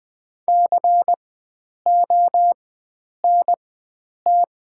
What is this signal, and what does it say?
Morse code, heard as a keyed tone, about 20 wpm, spaced as for 9 wpm, 700 Hz. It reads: CONT
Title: 4x Character Spacing - Top 500 Words - 20wpm